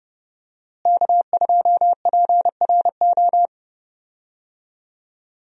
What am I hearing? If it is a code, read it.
Morse code, heard as a keyed tone, about 30 wpm, 700 Hz. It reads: K2PRO
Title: USA Call Signs - 5 Characters - Repeated 3x - 30wpm